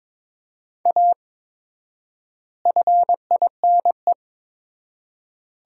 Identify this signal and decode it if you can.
Morse code, heard as a keyed tone, about 22 wpm, 700 Hz. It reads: A FINE
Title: 4x Word-Spacing - Sentences from Top 500 Words - 22wpm